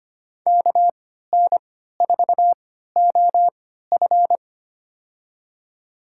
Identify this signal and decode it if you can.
Morse code, heard as a keyed tone, about 25 wpm, spaced as for 14 wpm, 700 Hz. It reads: KN4OF
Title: USA Call Signs - 3x Character Spacing - 25wpm